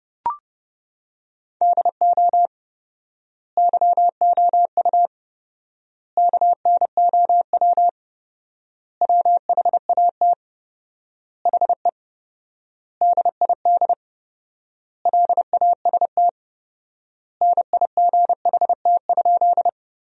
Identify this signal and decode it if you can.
Morse code, heard as a keyed tone, about 30 wpm, 700 Hz. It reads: DO YOU KNOW WHAT HE DID LAST NIGHT?
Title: 4x Word-Spacing - Sentences from Top 300 Words - 30wpm